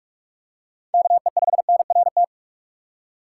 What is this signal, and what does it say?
Morse code, heard as a keyed tone, about 46 wpm, 700 Hz. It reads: KE5NRT
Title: USA Call Signs - 6 Characters - 2x Character Spacing - 45wpm